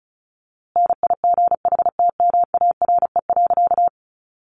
Morse code, heard as a keyed tone, about 34 wpm, 700 Hz. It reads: NIGHTMARE.
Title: A Sound Of Thunder 35wpm